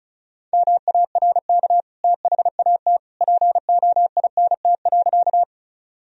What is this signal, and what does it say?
Morse code, heard as a keyed tone, about 34 wpm, 700 Hz. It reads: MARK THAT POINT.
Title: Sentences from Top 1000 Words - Review Part 1 - 35wpm